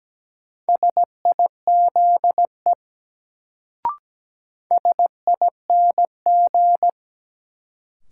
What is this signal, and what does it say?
Morse code, heard as a keyed tone, about 17 wpm, 700 Hz. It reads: SIZE SING
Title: Sentences from Top 400 Words - New Word Drill - 17wpm